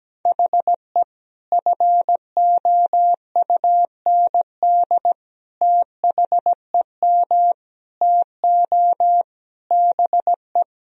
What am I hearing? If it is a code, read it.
Morse code, heard as a keyed tone, about 17 wpm, 700 Hz. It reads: HE FOUND THEM TO BE
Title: Sentences from Top 1000 Words - Review Part 3 - 17wpm